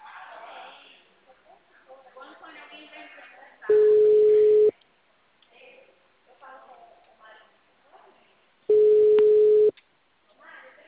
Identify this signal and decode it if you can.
Ringback tone (Europe, 425 Hz), with speech.